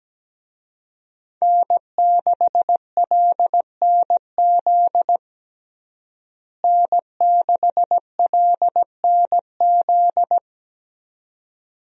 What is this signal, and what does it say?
Morse code, heard as a keyed tone, about 17 wpm, 700 Hz. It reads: N6LNZ N6LNZ
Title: USA Call Signs - 5 Characters - Repeated 2x - 17wpm